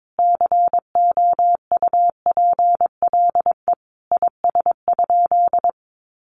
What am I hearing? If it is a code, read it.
Morse code, heard as a keyed tone, about 22 wpm, 700 Hz. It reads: COUPLE IS?
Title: Sentences from Top 1100 Words 22wpm